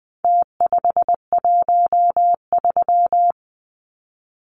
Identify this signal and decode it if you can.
Morse code, heard as a keyed tone, about 20 wpm, 700 Hz. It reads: T513
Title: Numbers - Contest Style - 4-Digit 20wpm